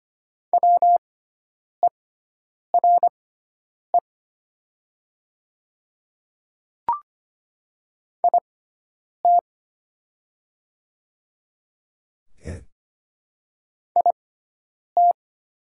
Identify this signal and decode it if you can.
Morse code, heard as a keyed tone, about 25 wpm, spaced as for 9 wpm, 700 Hz. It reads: WERE IT IT
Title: 6x Character Spacing - Top 100 Words - 25wpm